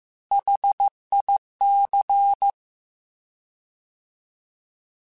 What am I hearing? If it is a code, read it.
Morse code, heard as a keyed tone, about 15 wpm, 800 Hz. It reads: HIC
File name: Alphabetical / Alphabetical mp3s / Hs @13